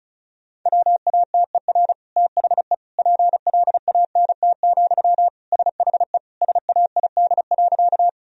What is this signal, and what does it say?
Morse code, heard as a keyed tone, about 34 wpm, 700 Hz. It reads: WATER THE PLANT, SHE SAID.
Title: Sentences from Top 1000 Words - Review Part 1 - 35wpm